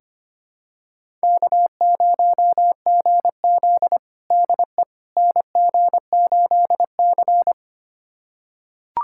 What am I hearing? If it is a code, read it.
Morse code, heard as a keyed tone, about 25 wpm, 700 Hz. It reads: K0GZ DE NG8C
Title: USA Callsign DE Callsign - 4 Characters - 25wpm